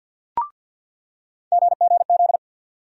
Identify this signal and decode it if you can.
Morse code, heard as a keyed tone, about 47 wpm, 700 Hz. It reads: GGZ
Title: USA Call Sign Postfixes - 3 Letters 50wpm